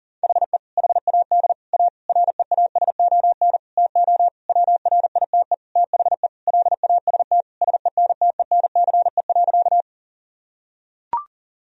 Morse code, heard as a keyed tone, about 37 wpm, 700 Hz. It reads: HE HAD A REASON TO WRITE THE LAST SENTENCE.
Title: Sentences from Top 1000 Words - Review Part 1 - 40wpm